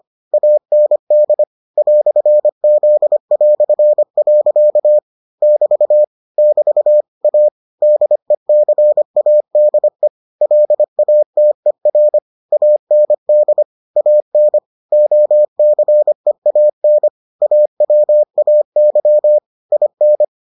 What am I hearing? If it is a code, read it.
Morse code, heard as a keyed tone, about 25 wpm, 600 Hz. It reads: AND "Z". <BT> <BT> A DECADE LATER AND AN OCEAN AWAY IN